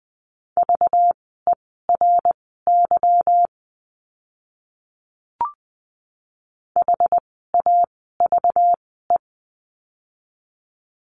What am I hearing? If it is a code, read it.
Morse code, heard as a keyed tone, about 20 wpm, spaced as for 14 wpm, 700 Hz. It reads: VERY HAVE
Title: 237 Lesson 38f - Letter q - Words 20wpm